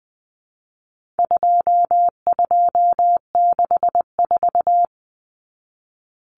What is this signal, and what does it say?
Morse code, heard as a keyed tone, about 20 wpm, 700 Hz. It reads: 2264
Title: Numbers - 4-Digit 20wpm